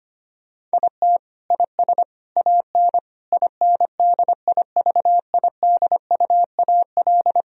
Morse code, heard as a keyed tone, about 25 wpm, 700 Hz. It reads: IT IS AN INDIVIDUAL
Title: Sentences from Top 1000 Words - Review Part 4 - 25wpm